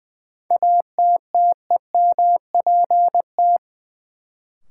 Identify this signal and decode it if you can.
Morse code, heard as a keyed tone, about 20 wpm, 700 Hz. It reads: ATTEMPT
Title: Sentences from Top 1000 Words - New Word Drill - 20wpm